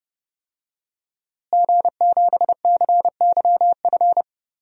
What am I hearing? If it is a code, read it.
Morse code, heard as a keyed tone, about 30 wpm, 700 Hz. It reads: G7CYF
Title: DXCC-100 Call Signs - 5 Characters - Repeated 2x - 30wpm